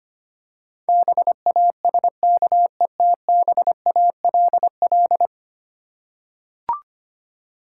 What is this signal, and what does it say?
Morse code, heard as a keyed tone, about 25 wpm, 700 Hz. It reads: BASKETBALL
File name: practice / 25wpm / Difficult Words with Hints - 25wpm